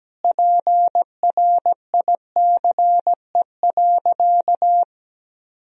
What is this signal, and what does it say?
Morse code, heard as a keyed tone, about 17 wpm, 700 Hz. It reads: PRICE.
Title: Sentences from Top 1000 Words - Review Part 4 - 17wpm